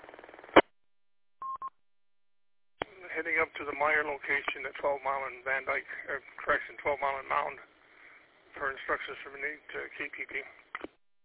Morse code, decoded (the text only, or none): none